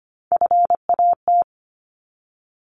Morse code, heard as a keyed tone, about 25 wpm, 700 Hz. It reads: FAT
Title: Top 501-750 Words 25wpm